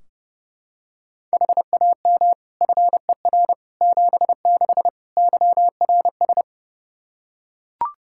Morse code, heard as a keyed tone, about 32 wpm, 700 Hz. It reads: HAM FER 76 YRS
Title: K1USN SST QSO Practice 30wpm